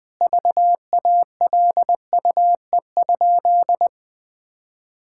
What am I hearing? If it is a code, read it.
Morse code, heard as a keyed tone, about 21 wpm, 700 Hz. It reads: VALUE?
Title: Sentences from Top 1000 Words - Review Part 6 - 20wpm